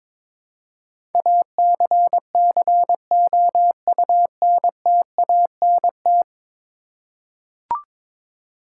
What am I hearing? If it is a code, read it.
Morse code, heard as a keyed tone, about 22 wpm, 700 Hz. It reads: ACCOUNTANT
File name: practice / 22wpm / 10 Letter Words - First 3 Letters Given - 22wpm